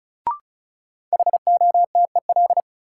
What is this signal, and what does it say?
Morse code, heard as a keyed tone, about 36 wpm, 700 Hz. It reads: HOTEL
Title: Sentences from Top 1600 Words - New Word Drill Alt - 35wpm